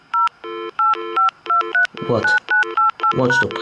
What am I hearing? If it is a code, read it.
DTMF, heard as telephone touch tones, tones of about 140 ms, with speech. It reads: 0 85239988#